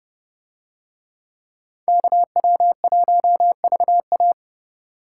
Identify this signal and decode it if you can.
Morse code, heard as a keyed tone, about 29 wpm, 700 Hz. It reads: KW1VA